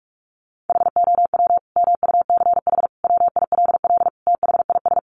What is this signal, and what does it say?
Morse code, heard as a keyed tone, about 44 wpm, 700 Hz. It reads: HOW MUCH WILL THIS
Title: Sentences from Top 1400 Words 45wpm